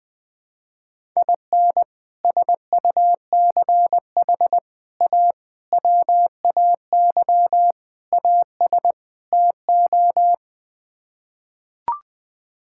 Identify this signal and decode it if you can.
Morse code, heard as a keyed tone, about 20 wpm, 700 Hz. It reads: IN SUCH A WAY AS TO
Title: Sets of 6 Words - 20wpm